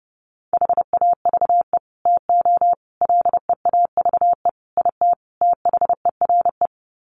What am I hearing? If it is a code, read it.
Morse code, heard as a keyed tone, about 30 wpm, 700 Hz. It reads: HAVE TO LEAVE IT THERE
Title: Sets of 5 Words Encore2 30wpm